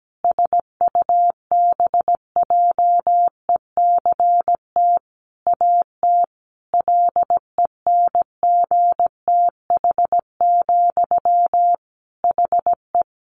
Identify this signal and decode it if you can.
Morse code, heard as a keyed tone, about 17 wpm, 700 Hz. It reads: SUBJECT AT LENGTH, HE
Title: Sentences from Top 1000 Words 17wpm